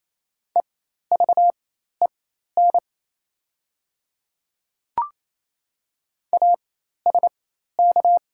Morse code, heard as a keyed tone, about 28 wpm, spaced as for 13 wpm, 700 Hz. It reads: EVEN ASK
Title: 4x Character Spacing - Top 300 Words - 28wpm